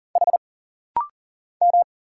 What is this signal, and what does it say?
Morse code, heard as a keyed tone, about 40 wpm, 700 Hz. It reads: H M
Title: Single Letter-Number - Rapid-Fire 40wpm